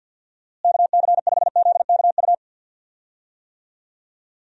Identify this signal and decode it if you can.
Morse code, heard as a keyed tone, about 53 wpm, 700 Hz. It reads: KC5ZKU
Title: USA Call Signs - 6 Characters - Repeated 3x - 50wpm